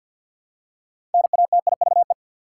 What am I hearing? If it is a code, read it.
Morse code, heard as a keyed tone, about 53 wpm, 700 Hz. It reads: NATIVE
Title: Six Letter Words - 50wpm